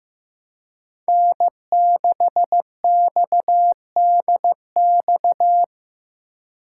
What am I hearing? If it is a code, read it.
Morse code, heard as a keyed tone, about 15 wpm, 700 Hz. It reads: N6XDX
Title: USA Call Signs - 2x Character Spacing - 15wpm